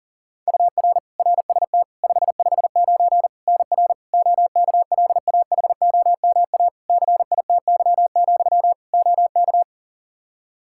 Morse code, heard as a keyed tone, about 40 wpm, 700 Hz. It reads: UR RST 559 NR OKLAHOMA CITY, OK
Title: Practice QSO - Exchange 1 - Beginning - 40wpm